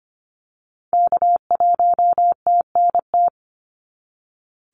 Morse code, heard as a keyed tone, about 25 wpm, 690 Hz. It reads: K1TNT
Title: NA SOTA Activator Callsigns 25wpm